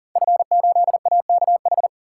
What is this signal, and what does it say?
Morse code, heard as a keyed tone, about 40 wpm, 700 Hz. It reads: F8AKH
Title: Code Groups - 5 Letter:Numbers - 40wpm